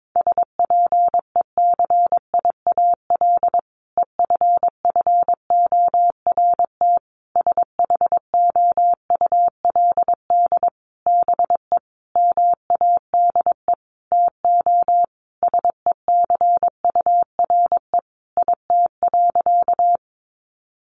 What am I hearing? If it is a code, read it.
Morse code, heard as a keyed tone, about 22 wpm, 700 Hz. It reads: SPECIAL EFFORT SHOULD BE MADE TO SECURE IT.
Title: Sentences from Top 1200 Words 22wpm